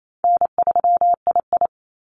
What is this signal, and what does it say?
Morse code, heard as a keyed tone, about 28 wpm, 700 Hz. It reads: N3II
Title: NA SOTA Activator Callsigns 28wpm